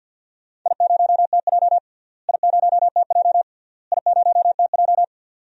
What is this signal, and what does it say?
Morse code, heard as a keyed tone, about 49 wpm, 700 Hz. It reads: I0TJ I0TJ I0TJ
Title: DXCC-100 Call Signs - 4 Characters - Repeated 3x - 50wpm